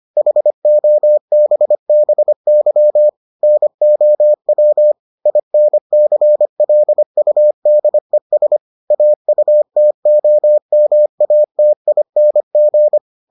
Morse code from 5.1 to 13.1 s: INCLUDES AUTOMATING

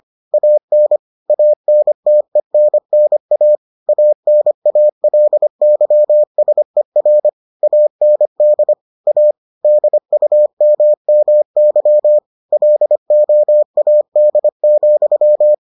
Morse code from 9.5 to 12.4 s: DUMMY

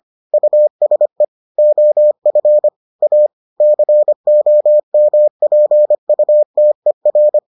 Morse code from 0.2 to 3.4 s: USE OF A

